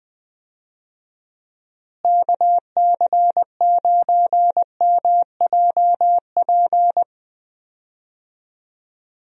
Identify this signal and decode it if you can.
Morse code, heard as a keyed tone, about 20 wpm, 700 Hz. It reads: KC9MJP
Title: USA Call Signs - 6 Characters - Repeated 2x - 20wpm